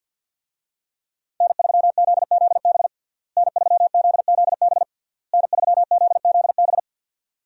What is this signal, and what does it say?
Morse code, heard as a keyed tone, about 52 wpm, 700 Hz. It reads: N3ZZB N3ZZB N3ZZB
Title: USA Call Signs - 5 Characters - Repeated 3x - 50wpm